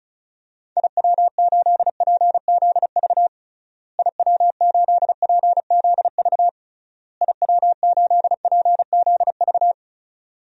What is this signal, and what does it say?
Morse code, heard as a keyed tone, about 36 wpm, 700 Hz. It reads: IW8PZV IW8PZV IW8PZV
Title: DXCC-100 Call Signs - 6 Characters - Repeated 3x - 35wpm